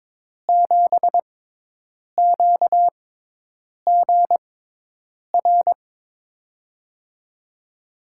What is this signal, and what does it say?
Morse code, heard as a keyed tone, about 22 wpm, spaced as for 8 wpm, 700 Hz. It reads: 7QGR
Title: USA Call Signs - 6 Characters - 6x Character Spacing - 22wpm